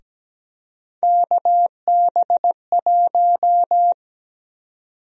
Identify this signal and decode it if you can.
Morse code, heard as a keyed tone, about 17 wpm, 700 Hz. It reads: KB1
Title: USA Call Sign Prefixes - Repeated 2x - 17wpm